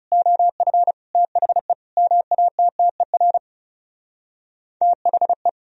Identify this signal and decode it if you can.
Morse code, heard as a keyed tone, about 40 wpm, 700 Hz. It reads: OF THE MATTER THE